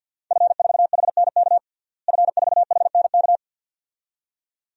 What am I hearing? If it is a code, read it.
Morse code, heard as a keyed tone, about 51 wpm, 700 Hz. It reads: F4HNK F4HNK
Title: DXCC-100 Call Signs - 5 Characters - Repeated 2x - 50wpm